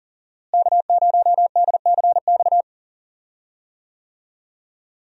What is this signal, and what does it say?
Morse code, heard as a keyed tone, about 40 wpm, 700 Hz. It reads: K0DCX
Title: USA Call Signs - 5 Characters - Repeated 2x - 40wpm